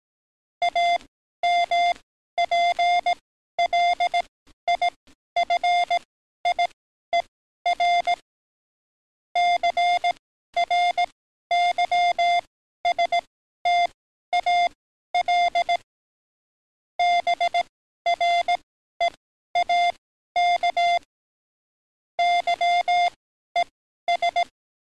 Morse code, heard as a keyed tone, about 18 wpm, spaced as for 12 wpm, 700 Hz. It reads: AMPLIFIER CRYSTAL BREAK YES
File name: training / NAVPERS 11049to11060 / NAVPERS 11053 RA RB